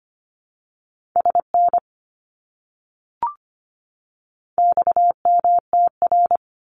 Morse code, heard as a keyed tone, about 25 wpm, 700 Hz. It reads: SN XMTR
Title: Abbreviations 25wpm